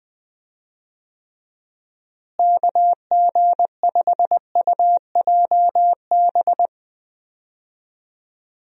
Morse code, heard as a keyed tone, about 20 wpm, 700 Hz. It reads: KG5UJB
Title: USA Call Signs - 6 Characters - Repeated 3x - 20wpm